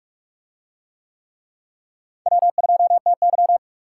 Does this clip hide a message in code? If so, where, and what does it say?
Yes: Morse code from 2.3 to 3.6 s, heard as a keyed tone, about 41 wpm, 700 Hz. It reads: W2TY